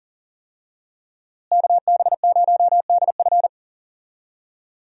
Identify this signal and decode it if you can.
Morse code, heard as a keyed tone, about 40 wpm, 700 Hz. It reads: KB0DF
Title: USA Call Signs - 5 Characters - Repeated 2x - 40wpm